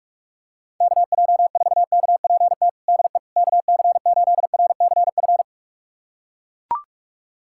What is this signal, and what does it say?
Morse code, heard as a keyed tone, about 41 wpm, 700 Hz. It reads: KJ4KPT DE KC8RCF
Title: USA Callsign DE Callsign - 6 Characters - 45wpm